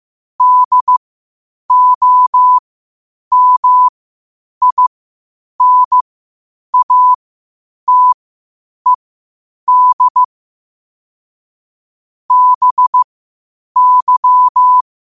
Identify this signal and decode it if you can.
Morse code, heard as a keyed tone, about 15 wpm, spaced as for 8 wpm, 1000 Hz. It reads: DOMINATED BY
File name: qotdfiles / 8wpm